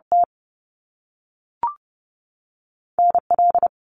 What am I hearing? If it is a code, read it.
Morse code, heard as a keyed tone, about 30 wpm, 700 Hz. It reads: T NL
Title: Canadian Provinces and Territories 30wpm